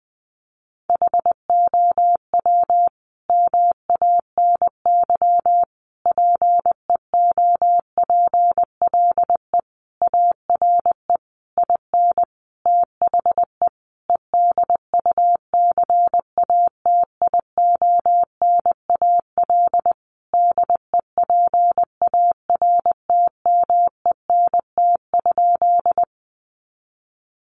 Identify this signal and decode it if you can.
Morse code, heard as a keyed tone, about 20 wpm, 700 Hz. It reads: HOW MANY PEOPLE ARE IN THE EDUCATIONAL DEPARTMENT?
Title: Sentences from Top 1200 Words 20wpm